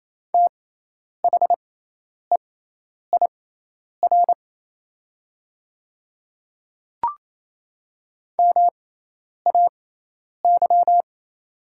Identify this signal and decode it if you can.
Morse code, heard as a keyed tone, about 28 wpm, spaced as for 10 wpm, 700 Hz. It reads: THEIR MAY
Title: 6x Character Spacing - Top 100 Words - 28wpm